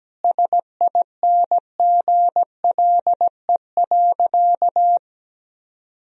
Morse code, heard as a keyed tone, about 17 wpm, 700 Hz. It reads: SINGLE.
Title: Sentences from Top 1000 Words - Review Part 4 - 17wpm